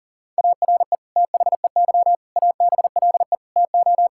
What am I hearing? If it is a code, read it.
Morse code, heard as a keyed tone, about 40 wpm, 700 Hz. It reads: ARE THEY ABLE TO
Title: Sentences from Top 1000 Words - Review Part 5 - 40wpm